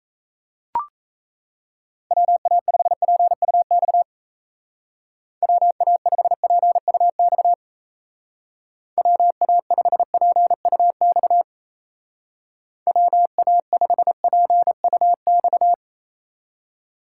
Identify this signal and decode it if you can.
Morse code, heard as a keyed tone, about 34 wpm, 700 Hz. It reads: WA5PUX WA5PUX WA5PUX WA5PUX
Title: Speed Racing - USA 2x3 Call Signs 28wpm